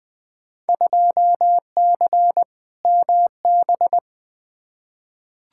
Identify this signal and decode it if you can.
Morse code, heard as a keyed tone, about 20 wpm, 700 Hz. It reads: 2C MB
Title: Field Day Exchanges Encore - 20wpm